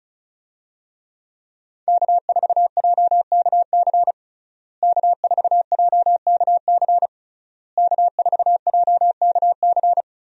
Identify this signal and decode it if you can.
Morse code, heard as a keyed tone, about 36 wpm, 700 Hz. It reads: K4JKC K4JKC K4JKC
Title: USA Call Signs - 5 Characters - Repeated 3x - 35wpm